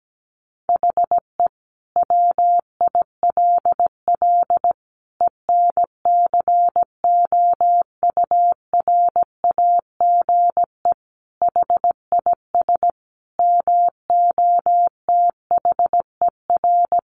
Morse code, heard as a keyed tone, about 17 wpm, 700 Hz. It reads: HE WILL ENCOURAGE HIS MOTHER
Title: Sentences from Top 1400 Words 17wpm